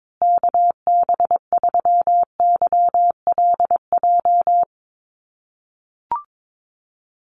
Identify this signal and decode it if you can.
Morse code, heard as a keyed tone, about 22 wpm, 700 Hz. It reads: KB3YLJ
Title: USA Call Signs Repeated 3x 22wpm